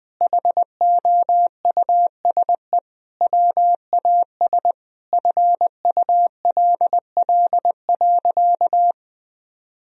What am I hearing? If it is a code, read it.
Morse code, heard as a keyed tone, about 20 wpm, 700 Hz. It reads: HOUSE WAS FULL.
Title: Sentences from Top 1000 Words - Review Part 3 - 20wpm